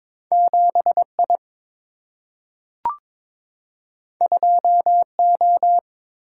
Morse code, heard as a keyed tone, about 22 wpm, 700 Hz. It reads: 7 I 2 O
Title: Single-Letter Single-Number 22wpm